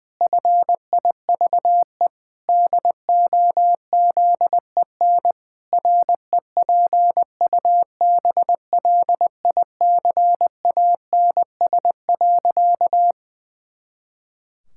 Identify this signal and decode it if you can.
Morse code, heard as a keyed tone, about 20 wpm, 700 Hz. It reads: FIVE DOZEN REPUBLICANS.
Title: Pangram Challenge v2 - 20wpm